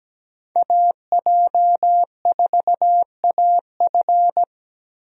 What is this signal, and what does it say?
Morse code, heard as a keyed tone, about 17 wpm, 700 Hz. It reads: AJ4AF
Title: USA Call Signs - 5 Characters - Repeated 3x - 17wpm